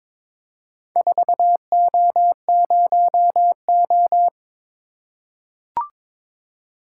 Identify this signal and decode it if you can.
Morse code, heard as a keyed tone, about 21 wpm, 700 Hz. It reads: 4O0O
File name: practice / 22wpm / Four Letter-Number Combinations - 22wpm